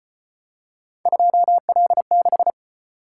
Morse code, heard as a keyed tone, about 33 wpm, 700 Hz. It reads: 2L6